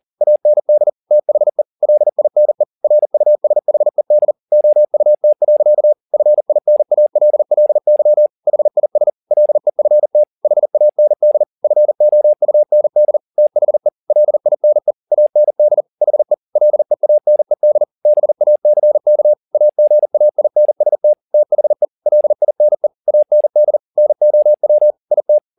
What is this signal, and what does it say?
Morse code, heard as a keyed tone, about 40 wpm, 600 Hz. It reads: AND THE LINE RUSHED OUT. FINALLY HIS LEFT HAND FOUND THE LINE AND HE LEANED BACK AGAINST THE LINE AND NOW IT
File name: the old man and the sea 2 40wpm